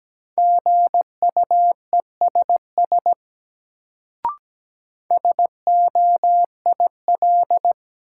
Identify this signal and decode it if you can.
Morse code, heard as a keyed tone, about 17 wpm, 700 Hz. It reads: GUESS SOIL
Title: Sentences from Top 1500 Words - New Word Drill Alt - 17wpm